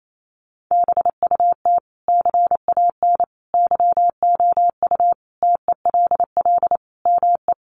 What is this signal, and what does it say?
Morse code, heard as a keyed tone, about 28 wpm, 700 Hz. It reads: BUT CAN YOU TELL ME
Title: Sentences from Top 1400 Words 28wpm